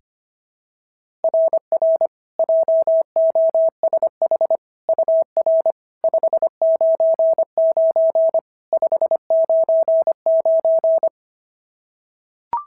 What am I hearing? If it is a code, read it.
Morse code, heard as a keyed tone, about 23 wpm, 650 Hz. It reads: RR JOSH UR 599 599
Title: SOTA QSO Practice 25wpm